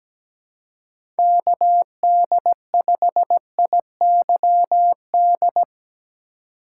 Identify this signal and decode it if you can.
Morse code, heard as a keyed tone, about 17 wpm, 700 Hz. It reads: KD5IYD
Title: USA Call Signs - 6 Characters - 2x Character Spacing - 17wpm